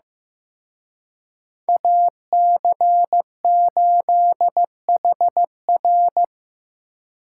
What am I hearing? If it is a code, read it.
Morse code, heard as a keyed tone, about 18 wpm, 700 Hz. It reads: AC8HR
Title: Speed Racing - USA 2x2 Call Signs 15wpm